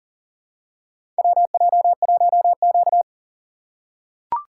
Morse code, heard as a keyed tone, about 40 wpm, 700 Hz. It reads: WJ1Q
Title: USA Call Signs - 4 Characters - 6x Character Spacing - 40wpm